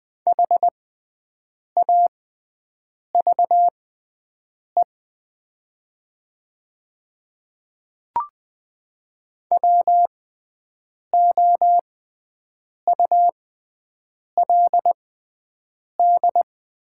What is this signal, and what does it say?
Morse code, heard as a keyed tone, about 20 wpm, spaced as for 7 wpm, 700 Hz. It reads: HAVE WOULD
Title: 6x Character Spacing - Top 100 Words - 20wpm